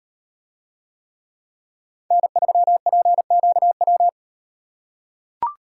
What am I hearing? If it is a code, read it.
Morse code, heard as a keyed tone, about 38 wpm, 700 Hz. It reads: N3PQW